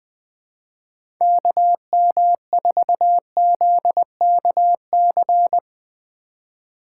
Morse code, heard as a keyed tone, about 20 wpm, 700 Hz. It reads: KM4ZKC
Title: USA Call Signs - 6 Characters - 4x Character Spacing - 20wpm